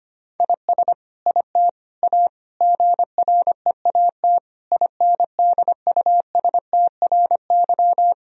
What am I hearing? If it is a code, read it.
Morse code, heard as a keyed tone, about 25 wpm, 700 Hz. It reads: IS IT A GREAT INDUSTRY
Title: Sentences from Top 1000 Words - Review Part 4 - 25wpm